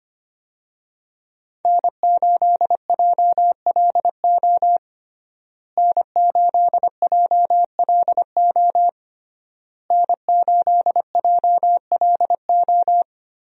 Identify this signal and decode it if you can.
Morse code, heard as a keyed tone, about 25 wpm, 700 Hz. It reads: N8JLO N8JLO N8JLO
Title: USA Call Signs - 5 Characters - Repeated 3x - 25wpm